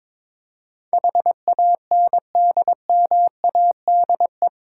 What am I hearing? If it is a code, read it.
Morse code, heard as a keyed tone, about 22 wpm, 700 Hz. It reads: HANDMADE
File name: Difficult Words with Hints - 22wpm